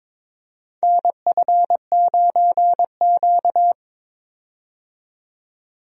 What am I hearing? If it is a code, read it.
Morse code, heard as a keyed tone, about 22 wpm, 700 Hz. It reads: NF9Q
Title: USA Call Signs - 4 Characters - 22wpm